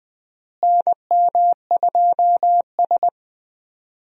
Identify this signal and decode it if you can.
Morse code, heard as a keyed tone, about 20 wpm, 700 Hz. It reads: NM2S
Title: USA Call Signs - 4 Characters - Repeated 3x - 20wpm